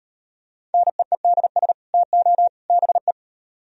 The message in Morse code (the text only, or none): NEEDS TO BE